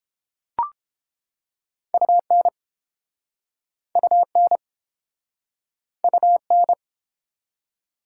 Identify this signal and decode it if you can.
Morse code, heard as a keyed tone, about 33 wpm, 700 Hz. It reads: UN UN UN